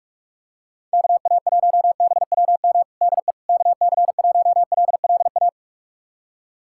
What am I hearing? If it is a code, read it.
Morse code, heard as a keyed tone, about 41 wpm, 700 Hz. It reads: KA1BWM DE KC1LLA
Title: USA Callsign DE Callsign 45wpm